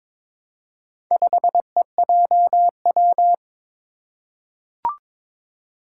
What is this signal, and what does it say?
Morse code, heard as a keyed tone, about 22 wpm, 700 Hz. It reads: 5EJW
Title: Four Challenging Letter-Number Combinations 22wpm